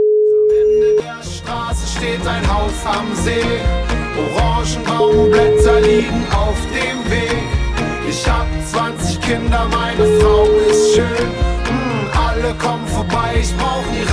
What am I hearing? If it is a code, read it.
Ringback tone (Europe, 425 Hz), with noise.